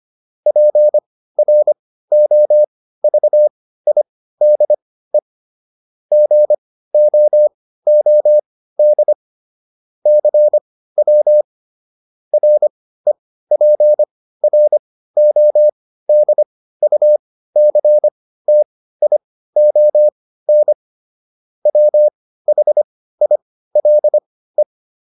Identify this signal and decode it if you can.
Morse code, heard as a keyed tone, about 25 wpm, spaced as for 15 wpm, 600 Hz. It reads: PROVIDE GOOD CW REPRODUCTION WHILE